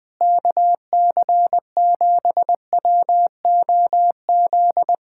Morse code, heard as a keyed tone, about 20 wpm, 700 Hz. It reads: KC7WOZ
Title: Speed-Racing Common QSO Phrases - 20wpm